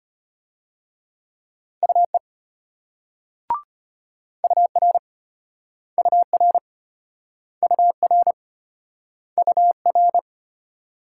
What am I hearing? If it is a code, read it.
Morse code, heard as a keyed tone, about 34 wpm, 700 Hz. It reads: UE UR UR UR UR